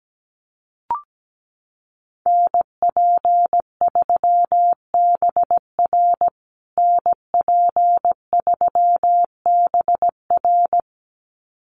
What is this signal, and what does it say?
Morse code, heard as a keyed tone, about 17 wpm, 700 Hz. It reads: NP3BR NP3BR
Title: USA Call Signs Repeated 2x 17wpm